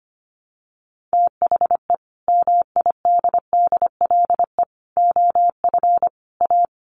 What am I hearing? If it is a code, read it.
Morse code, heard as a keyed tone, about 25 wpm, 700 Hz. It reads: THE MIDDLE OF A
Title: Sets of 4 Words 25wpm